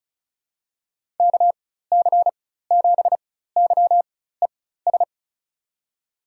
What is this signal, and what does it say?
Morse code, heard as a keyed tone, about 34 wpm, spaced as for 16 wpm, 700 Hz. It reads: KC7YES
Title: USA Call Signs - 4x Character Spacing - 35wpm